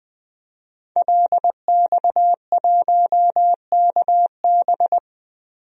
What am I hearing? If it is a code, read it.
Morse code, heard as a keyed tone, about 21 wpm, 700 Hz. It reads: LX1KB
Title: DXCC-100 Call Signs - 5 Characters - 20wpm